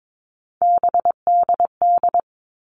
Morse code, heard as a keyed tone, about 22 wpm, 700 Hz. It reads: BDD
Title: All Cut Numbers - 3-Digit 22wpm